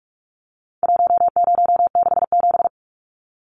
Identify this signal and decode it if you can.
Morse code, heard as a keyed tone, about 44 wpm, 700 Hz. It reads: 1067
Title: Numbers - 4-Digit 45wpm